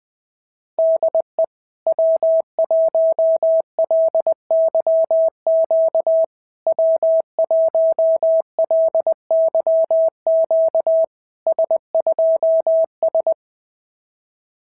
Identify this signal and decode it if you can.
Morse code, heard as a keyed tone, about 18 wpm, 650 Hz. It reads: DE W1LYQ W1LYQ S2S
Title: SOTA QSO Practice 20wpm